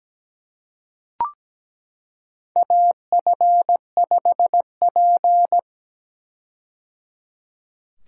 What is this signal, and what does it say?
Morse code, heard as a keyed tone, about 17 wpm, 700 Hz. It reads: AF5P
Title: K1USN SST QSO Practice - 17wpm